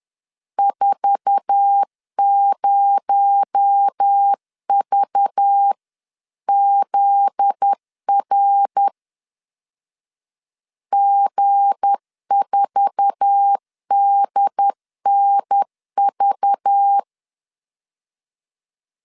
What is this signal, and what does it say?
Morse code, heard as a keyed tone, about 10 wpm, 800 Hz. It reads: 40VZR G4DNV
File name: LEZIONI / 8 YJ45 / 8 YJ4 10wpm 2633